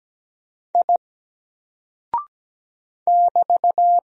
Morse code, heard as a keyed tone, about 17 wpm, 700 Hz. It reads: I <BT>
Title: Single Character (Letter:Number:Punctuation:Prosign) 17wpm